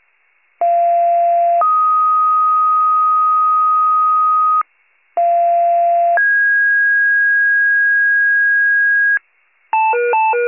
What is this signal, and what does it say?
Two-tone page, heard as a one-second tone, then a three-second tone, about 690 then 1250 Hz, about 690 then 1650 Hz, with noise. It is heard from 0.6 to 9.2 s.
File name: pager / 41 FIRE / wcalert